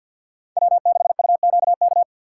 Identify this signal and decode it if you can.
Morse code, heard as a keyed tone, about 53 wpm, 700 Hz. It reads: W6UQK
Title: USA Call Signs - 5 Characters - 2x Character Spacing - 50wpm